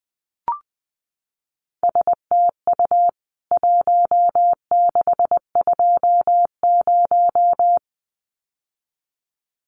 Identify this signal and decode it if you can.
Morse code, heard as a keyed tone, about 20 wpm, 700 Hz. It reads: STU 1620
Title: CWT Exchange 20wpm